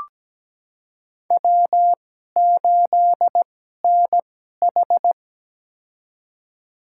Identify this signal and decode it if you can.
Morse code, heard as a keyed tone, about 17 wpm, spaced as for 12 wpm, 700 Hz. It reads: W8NH
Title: USA Call Signs - 4 Characters - 2x Character Spacing - 17wpm